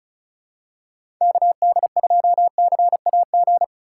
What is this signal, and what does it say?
Morse code, heard as a keyed tone, about 34 wpm, 700 Hz. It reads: KD2CAG
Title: USA Call Signs - 6 Characters - 3x Character Spacing - 35wpm